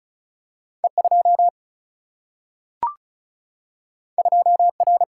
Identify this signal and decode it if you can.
Morse code, heard as a keyed tone, about 36 wpm, 700 Hz. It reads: E 2 2 R
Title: Single-Letter Single-Number 35wpm